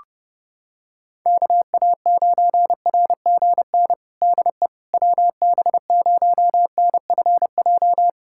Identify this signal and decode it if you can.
Morse code, heard as a keyed tone, about 30 wpm, 700 Hz. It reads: KA9RGN DE WB0NFJ
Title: USA Callsign DE Callsign - 6 Characters - 30wpm